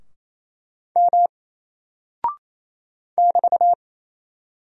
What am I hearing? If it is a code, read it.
Morse code, heard as a keyed tone, about 27 wpm, 700 Hz. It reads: M <BT>
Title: Single Character (Letter:Number:Punctuation:Prosign) Alt 28wpm